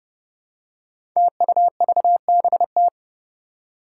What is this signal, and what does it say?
Morse code, heard as a keyed tone, about 30 wpm, 700 Hz. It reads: TUVBT
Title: Five Challenging Letter-Number Combinations 30wpm